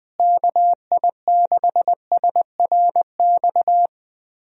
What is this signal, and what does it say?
Morse code, heard as a keyed tone, about 20 wpm, 700 Hz. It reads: KI6SRX
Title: USA Call Signs - 6 Characters - Repeated 2x - 20wpm